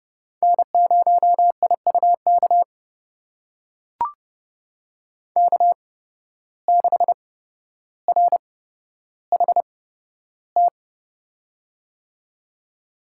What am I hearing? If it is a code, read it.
Morse code, heard as a keyed tone, about 30 wpm, spaced as for 13 wpm, 700 Hz. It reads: N0IUK K6RHT
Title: USA Call Signs - 5 Characters - 8x Character Spacing - 30wpm